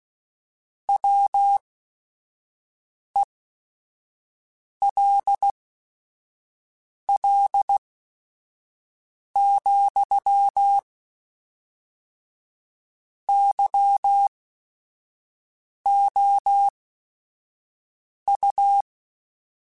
Morse code, heard as a keyed tone, about 16 wpm, spaced as for 5 wpm, 800 Hz. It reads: WELL, YOU
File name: Technical Subjects / Technical Subjects-mp3s / AND Gate @5